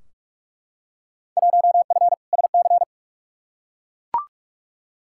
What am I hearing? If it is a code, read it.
Morse code, heard as a keyed tone, about 43 wpm, 700 Hz. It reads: 1F SC
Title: Field Day Exchanges Encore - 45wpm